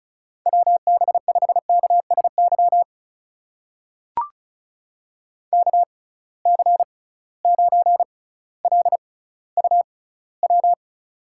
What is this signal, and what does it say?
Morse code, heard as a keyed tone, about 36 wpm, spaced as for 18 wpm, 700 Hz. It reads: WB5KSY KC9LUW
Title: USA Call Signs - 6 Characters - 6x Character Spacing - 35wpm